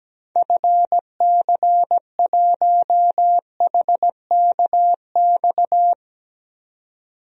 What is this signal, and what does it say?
Morse code, heard as a keyed tone, about 17 wpm, 700 Hz. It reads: FC1HKX
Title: DXCC-100 Call Signs - 6 Characters - 17wpm